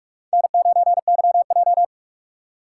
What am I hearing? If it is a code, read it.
Morse code, heard as a keyed tone, about 44 wpm, 700 Hz. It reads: N9YJ
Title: USA Call Signs - 6x Character Spacing - 45wpm